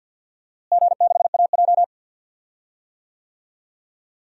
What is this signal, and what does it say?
Morse code, heard as a keyed tone, about 50 wpm, 700 Hz. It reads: G6AJ
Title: DXCC-100 Call Signs - 4 Characters - Repeated 2x - 50wpm